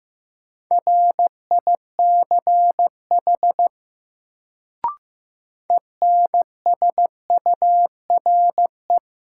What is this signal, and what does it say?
Morse code, heard as a keyed tone, about 16 wpm, 700 Hz. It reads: RICH ENSURE
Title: Sentences from Top 1100 Words - New Word Drill Alt - 15wpm